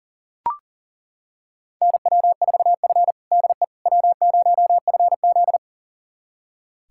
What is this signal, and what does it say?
Morse code, heard as a keyed tone, about 37 wpm, 700 Hz. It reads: NW4F DE W0FZ
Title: USA Callsign DE Callsign - 4 Characters - 40wpm